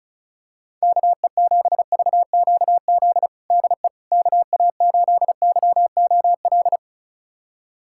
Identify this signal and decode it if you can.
Morse code, heard as a keyed tone, about 36 wpm, 700 Hz. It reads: KE7VQZ DE KA8YOL
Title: USA Callsign DE Callsign - 6 Characters - 35wpm